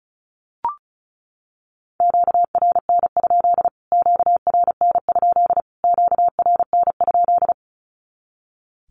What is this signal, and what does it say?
Morse code, heard as a keyed tone, about 34 wpm, 700 Hz. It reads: QRN? QRN? QRN?
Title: Q-Codes Repeated 3x 35wpm